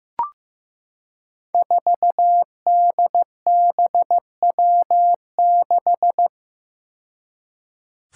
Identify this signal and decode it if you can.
Morse code, heard as a keyed tone, about 15 wpm, 700 Hz. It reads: 4DBW6
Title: Five Challenging Letter-Number Combinations 15wpm